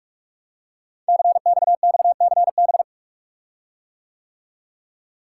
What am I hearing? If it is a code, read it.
Morse code, heard as a keyed tone, about 44 wpm, 700 Hz. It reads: CXXCB
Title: Code Groups - 5 Letters - 45wpm